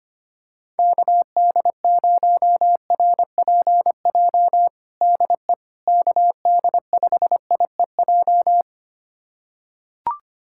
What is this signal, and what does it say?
Morse code, heard as a keyed tone, about 25 wpm, 700 Hz. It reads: KD0RPJ DE KD5IEJ
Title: USA Callsign DE Callsign - 6 Characters - 25wpm